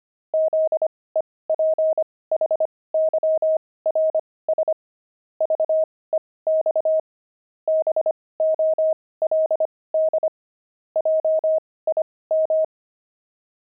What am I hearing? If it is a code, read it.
Morse code, heard as a keyed tone, about 25 wpm, spaced as for 18 wpm, 630 Hz. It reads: ZEPHYRS VEX BOLD JIM
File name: pangram-fs-18-cs-25